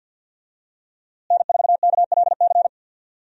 Morse code, heard as a keyed tone, about 53 wpm, 700 Hz. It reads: N4KLC
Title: USA Call Signs - 5 Characters - Repeated 2x - 50wpm